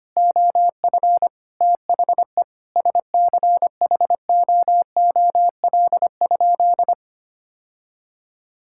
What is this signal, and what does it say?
Morse code, heard as a keyed tone, about 25 wpm, 700 Hz. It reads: OF THE SCHOOL?
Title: Sentences from Top 1000 Words - Review Part 5 - 25wpm